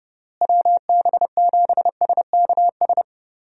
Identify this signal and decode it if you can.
Morse code, heard as a keyed tone, about 30 wpm, 700 Hz. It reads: WB7SKS
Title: USA Call Signs - 6 Characters - Repeated 3x - 30wpm